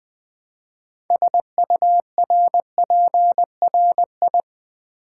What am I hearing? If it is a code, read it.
Morse code, heard as a keyed tone, about 20 wpm, 700 Hz. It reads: SURPRI
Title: Build A Word - 3 Letters At A Time - 9-Letter Words - 20wpm